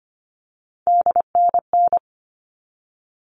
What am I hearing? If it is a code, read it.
Morse code, heard as a keyed tone, about 25 wpm, 700 Hz. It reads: DNN
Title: All Cut Numbers - 3-Digit 25wpm